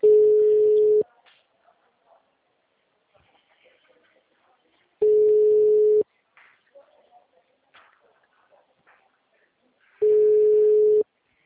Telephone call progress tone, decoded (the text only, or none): ringback tone